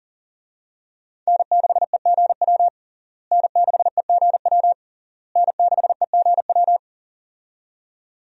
Morse code, heard as a keyed tone, about 40 wpm, 700 Hz. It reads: N6EGW N6EGW N6EGW
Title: USA Call Signs - 5 Characters - Repeated 3x - 40wpm